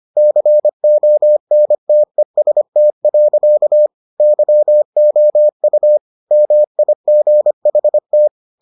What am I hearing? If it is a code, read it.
Morse code, heard as a keyed tone, about 25 wpm, 600 Hz. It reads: CONTEST. YOU MIGHT